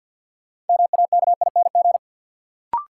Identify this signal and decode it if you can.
Morse code, heard as a keyed tone, about 50 wpm, 700 Hz. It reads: MAKING
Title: Sentences from Top 1000 Words - Review Part 6 - New Word Drill - 50wpm